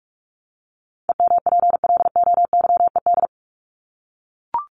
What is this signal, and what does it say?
Morse code, heard as a keyed tone, about 44 wpm, 700 Hz. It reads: EMPLOYED
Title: ED Suffix - Top 100 Words with Base Words 45wpm